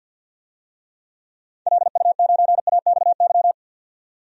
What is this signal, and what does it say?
Morse code, heard as a keyed tone, about 47 wpm, 700 Hz. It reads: LU9AXY
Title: DXCC-100 Call Signs - 6 Characters - 50wpm